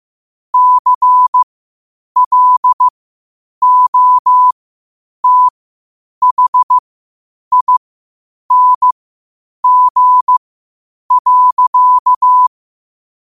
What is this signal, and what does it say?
Morse code, heard as a keyed tone, about 15 wpm, spaced as for 8 wpm, 1000 Hz. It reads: CLOTHING.
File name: qotdfiles / 8wpm